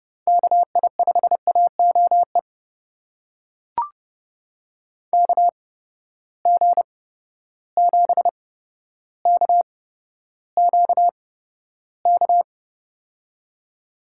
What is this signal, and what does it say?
Morse code, heard as a keyed tone, about 30 wpm, spaced as for 13 wpm, 700 Hz. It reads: KI5AOE KG7KQK
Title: USA Call Signs - 8x Character Spacing - 30wpm